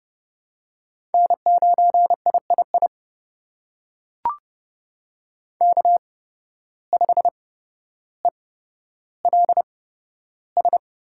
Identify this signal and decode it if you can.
Morse code, heard as a keyed tone, about 31 wpm, spaced as for 13 wpm, 700 Hz. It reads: N9III K5ELS
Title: USA Call Signs - 5 Characters - 8x Character Spacing - 30wpm